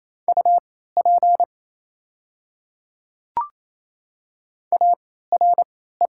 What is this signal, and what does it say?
Morse code, heard as a keyed tone, about 28 wpm, spaced as for 16 wpm, 700 Hz. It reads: UP ARE
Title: 3x Character Spacing - Top 100 Words - 28wpm